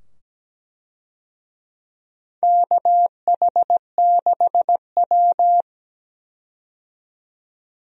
Morse code, heard as a keyed tone, about 17 wpm, 700 Hz. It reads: KH6W
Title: DXCC-100 Call Signs - 4 Characters - 17wpm